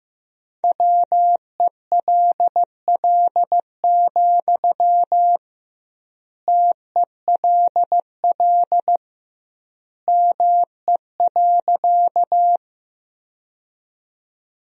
Morse code, heard as a keyed tone, about 15 wpm, 700 Hz. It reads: WELL, TELL ME.
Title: 2x Word-Spacing - Sentences from Top 200 Words - 15wpm